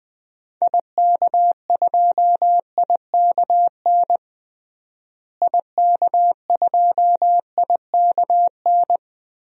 Morse code, heard as a keyed tone, about 20 wpm, 700 Hz. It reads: IK2IKN IK2IKN
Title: DXCC-100 Call Signs - 6 Characters - Repeated 3x - 20wpm